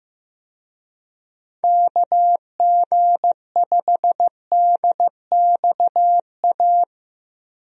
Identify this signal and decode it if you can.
Morse code, heard as a keyed tone, about 15 wpm, 700 Hz. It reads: KG5DXA
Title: USA Call Signs - 6 Characters - Repeated 2x - 15wpm